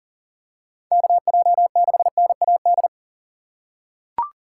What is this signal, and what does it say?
Morse code, heard as a keyed tone, about 40 wpm, 700 Hz. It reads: KJ6NAD
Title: USA Call Signs - 6 Characters - 2x Character Spacing - 40wpm